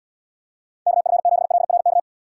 Morse code, heard as a keyed tone, about 78 wpm, 700 Hz. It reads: KF7LRK